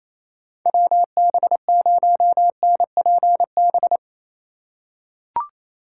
Morse code, heard as a keyed tone, about 28 wpm, 700 Hz. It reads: WB0NPB
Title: USA Call Signs - 6 Characters - 4x Character Spacing - 28wpm